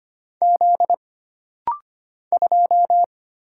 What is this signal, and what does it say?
Morse code, heard as a keyed tone, about 25 wpm, 700 Hz. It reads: Z 2
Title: Single Letter-Number - Rapid-Fire 25wpm